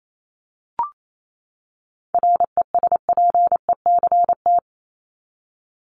Morse code, heard as a keyed tone, about 28 wpm, 700 Hz. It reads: RESPECT
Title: Top 1501-1750 Words 28wpm